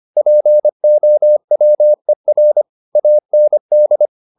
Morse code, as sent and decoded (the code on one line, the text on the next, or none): .--. --- .-- . .-. / .- -. -..
POWER AND